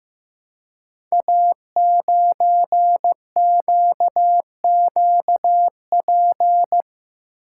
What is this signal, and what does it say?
Morse code, heard as a keyed tone, about 15 wpm, 700 Hz. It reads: A9QQP
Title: Code Groups - 5 Letter:Numbers - 15wpm